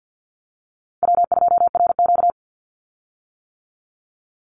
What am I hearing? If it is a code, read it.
Morse code, heard as a keyed tone, about 50 wpm, 700 Hz. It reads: W2RQ
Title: Easy Call Signs - 50wpm